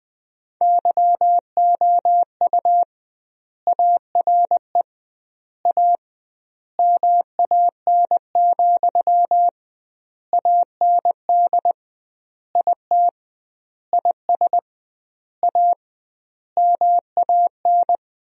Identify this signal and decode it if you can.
Morse code, heard as a keyed tone, about 20 wpm, 700 Hz. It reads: YOU ARE A MAN, AND IT IS A MAN
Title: 2x Word-Spacing - Sentences from Top 200 Words - 20wpm